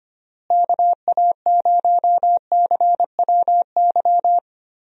Morse code, heard as a keyed tone, about 25 wpm, 700 Hz. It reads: KA0CWY
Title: USA Call Signs - 6 Characters - Repeated 3x - 25wpm